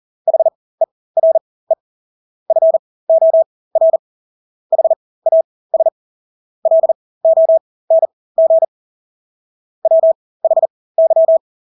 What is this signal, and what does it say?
Morse code, heard as a keyed tone, about 40 wpm, spaced as for 20 wpm, 650 Hz. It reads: HERE FOR HAS LONG WHY
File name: Engelske ord 40-20 wpm